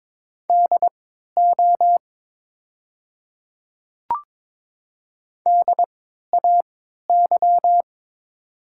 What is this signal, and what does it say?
Morse code, heard as a keyed tone, about 22 wpm, spaced as for 13 wpm, 700 Hz. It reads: DO DAY
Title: 3x Character Spacing - Top 500 Words - 22wpm